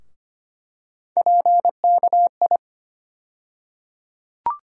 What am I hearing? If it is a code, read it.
Morse code, heard as a keyed tone, about 25 wpm, 700 Hz. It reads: PKI
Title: USA Call Sign Postfixes - 3 Letters 25wpm